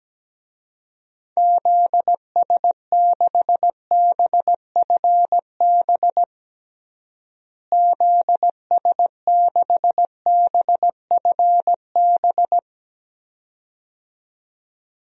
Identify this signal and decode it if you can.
Morse code, heard as a keyed tone, about 17 wpm, 700 Hz. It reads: ZS6BFB ZS6BFB
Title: DXCC-100 Call Signs - 6 Characters - Repeated 2x - 17wpm